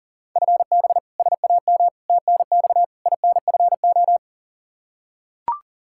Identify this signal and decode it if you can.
Morse code, heard as a keyed tone, about 40 wpm, 690 Hz. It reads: FB SAM TNX INFO
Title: Practice QSO - Exchange 3 - More Info - 40wpm